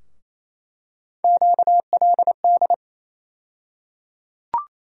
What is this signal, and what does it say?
Morse code, heard as a keyed tone, about 28 wpm, 700 Hz. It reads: QLD
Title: USA Call Sign Postfixes - 3 Letters 28wpm